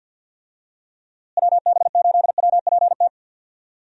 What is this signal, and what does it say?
Morse code, heard as a keyed tone, about 50 wpm, 700 Hz. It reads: WB8WPT
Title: USA Call Signs - 6 Characters - Repeated 3x - 50wpm